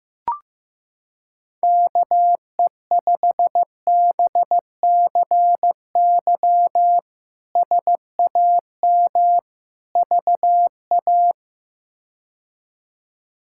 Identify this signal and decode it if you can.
Morse code, heard as a keyed tone, about 15 wpm, 700 Hz. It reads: KE5BCY SAM VA
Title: K1USN SST QSO Practice - 15wpm